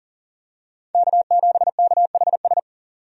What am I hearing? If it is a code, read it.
Morse code, heard as a keyed tone, about 40 wpm, 700 Hz. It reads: K7KHS
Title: USA Call Signs - 5 Characters - 40wpm